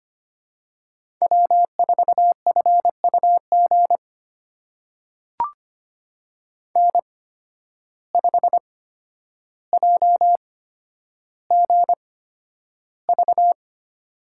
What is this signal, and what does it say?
Morse code, heard as a keyed tone, about 25 wpm, spaced as for 11 wpm, 700 Hz. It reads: W4FUG N5JGV
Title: USA Call Signs - 5 Characters - 8x Character Spacing - 25wpm